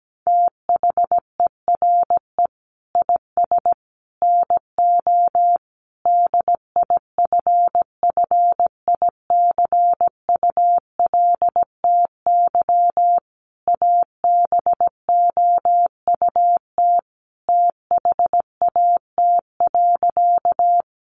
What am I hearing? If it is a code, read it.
Morse code, heard as a keyed tone, about 17 wpm, 700 Hz. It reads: THERE IS NO DIFFICULTY ABOUT THAT.
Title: Sentences from Top 1500 Words 17wpm